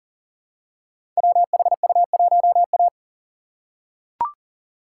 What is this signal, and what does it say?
Morse code, heard as a keyed tone, about 39 wpm, 690 Hz. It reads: WHU1A
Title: Code Groups - 5 Letter:Numbers - 40wpm